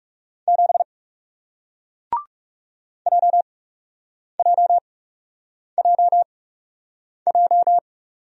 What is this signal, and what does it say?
Morse code, heard as a keyed tone, about 41 wpm, spaced as for 8 wpm, 700 Hz. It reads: 7 JJJJ